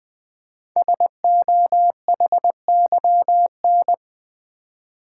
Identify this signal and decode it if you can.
Morse code, heard as a keyed tone, about 20 wpm, 700 Hz. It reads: SOHYN
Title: Code Groups - 5 Letters - 20wpm